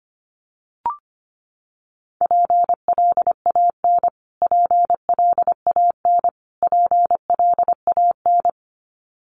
Morse code, heard as a keyed tone, about 25 wpm, 700 Hz. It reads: PLAN PLAN PLAN
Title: Top 500 Words Repeated 25wpm